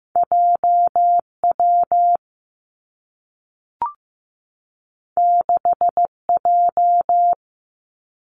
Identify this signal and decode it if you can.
Morse code, heard as a keyed tone, about 15 wpm, 700 Hz. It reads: J W 6 J
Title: Two Challenging Letter-Number Combinations 15wpm